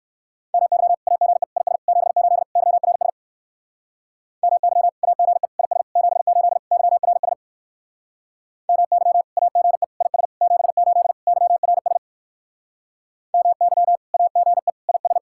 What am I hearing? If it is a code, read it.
Morse code, heard as a keyed tone, about 59 wpm, 700 Hz. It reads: MY AGE IS 78 YRS MY AGE IS 78 YRS MY AGE IS 78 YRS MY AGE IS